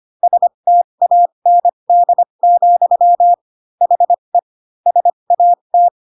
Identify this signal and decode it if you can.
Morse code, heard as a keyed tone, about 25 wpm, spaced as for 22 wpm, 700 Hz. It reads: STAND, HE SAT
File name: Pinocchio / Pinocchio ch6 18 25 ik1hge